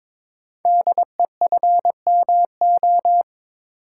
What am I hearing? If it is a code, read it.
Morse code, heard as a keyed tone, about 22 wpm, 700 Hz. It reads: DEFMO
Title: Code Groups - 5 Letters - 22wpm